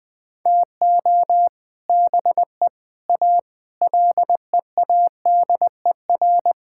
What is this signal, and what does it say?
Morse code, heard as a keyed tone, about 20 wpm, 700 Hz. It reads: TO BE A LEADER
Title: Sentences with 3-Word Phrases - 20wpm